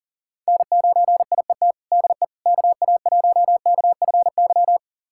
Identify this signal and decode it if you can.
Morse code, heard as a keyed tone, about 40 wpm, 700 Hz. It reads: N9IET DE KA1KFY
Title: USA Callsign DE Callsign 40wpm